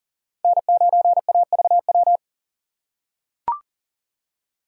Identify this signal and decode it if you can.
Morse code, heard as a keyed tone, about 40 wpm, 700 Hz. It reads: N9AVW
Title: USA Call Signs - 5 Characters - 3x Character Spacing - 40wpm